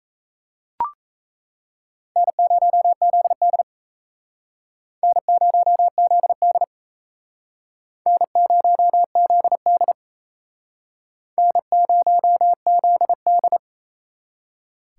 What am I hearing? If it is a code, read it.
Morse code, heard as a keyed tone, about 34 wpm, 700 Hz. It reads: N0ZD N0ZD N0ZD N0ZD
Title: Speed Racing - USA 1x2 Call Signs 28wpm